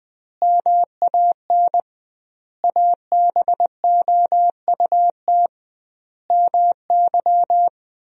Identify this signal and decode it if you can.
Morse code, heard as a keyed tone, about 20 wpm, 700 Hz. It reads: MAN ABOUT MY
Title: 2x Word-Spacing - Sentences from Top 400 Words - 20wpm